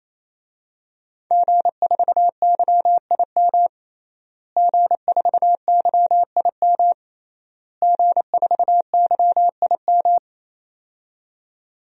Morse code, heard as a keyed tone, about 28 wpm, 700 Hz. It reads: G4YIM G4YIM G4YIM
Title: DXCC-100 Call Signs - 5 Characters - Repeated 3x - 28wpm